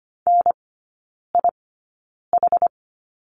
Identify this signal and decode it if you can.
Morse code, heard as a keyed tone, about 25 wpm, spaced as for 9 wpm, 700 Hz. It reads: NIH
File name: course / 25:10wpm / 093 Lesson 16e - Letter c - Three Characters 25wpm